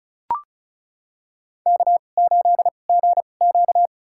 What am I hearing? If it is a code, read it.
Morse code, heard as a keyed tone, about 36 wpm, spaced as for 25 wpm, 700 Hz. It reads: K8GQ
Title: USA Call Signs - 4 Characters - 2x Character Spacing - 35wpm